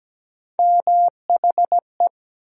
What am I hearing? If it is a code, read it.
Morse code, heard as a keyed tone, about 17 wpm, 700 Hz. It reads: MHE
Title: USA Call Sign Postfixes - 3 Letters 17wpm